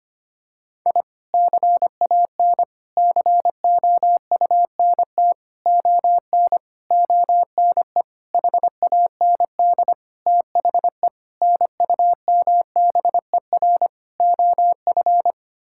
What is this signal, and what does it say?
Morse code, heard as a keyed tone, about 25 wpm, 700 Hz. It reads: I CAN COUNT ON ONE HAND THE NUMBER OF
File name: Sets of 9 Words - 25wpm